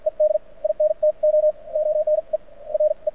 Morse code, heard as a keyed tone, about 53 wpm, spaced as for 43 wpm, 610 Hz. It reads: ED INTO ONE RE
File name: archived / CRSnet